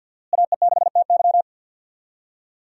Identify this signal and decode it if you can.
Morse code, heard as a keyed tone, about 51 wpm, 700 Hz. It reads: AE6TY
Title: USA Call Signs - 5 Characters - Repeated 2x - 50wpm